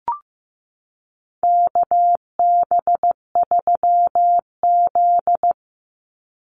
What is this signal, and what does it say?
Morse code, heard as a keyed tone, about 15 wpm, 700 Hz. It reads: KB3Z
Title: Call Signs Repeated 2x - 15wpm